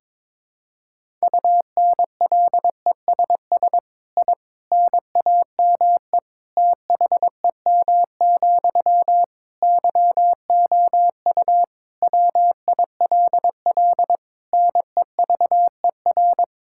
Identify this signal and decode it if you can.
Morse code, heard as a keyed tone, about 22 wpm, 700 Hz. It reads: UNLESS I NAME THEM, YOU WILL NEVER
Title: Sentences from Top 1000 Words - Review Part 6 - 22wpm